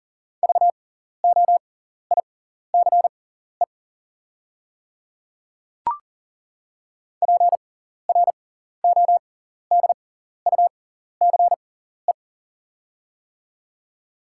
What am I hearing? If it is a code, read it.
Morse code, heard as a keyed tone, about 40 wpm, spaced as for 14 wpm, 690 Hz. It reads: VOICE PRODUCE
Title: 6x Character Spacing - Top 500 Words - 40wpm